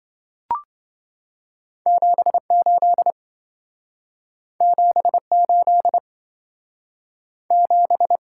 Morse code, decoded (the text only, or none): 78 78 7